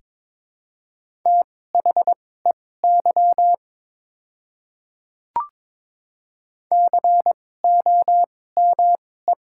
Morse code, heard as a keyed tone, about 22 wpm, spaced as for 16 wpm, 700 Hz. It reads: THEY COME
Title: 2x Character Spacing - Top 100 Words - 22wpm